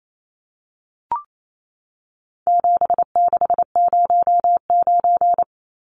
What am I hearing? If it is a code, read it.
Morse code, heard as a keyed tone, about 28 wpm, 700 Hz. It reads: 7609
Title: Numbers - 4-Digit 28wpm